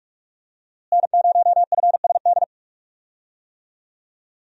Morse code, heard as a keyed tone, about 44 wpm, 700 Hz. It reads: N0FSD
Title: USA Call Signs - 5 Characters - Repeated 3x - 45wpm